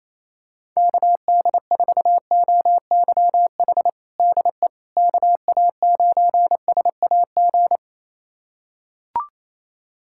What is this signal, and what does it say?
Morse code, heard as a keyed tone, about 27 wpm, 700 Hz. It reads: KD4OYH DE KA9SAG
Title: USA Callsign DE Callsign - 6 Characters - 28wpm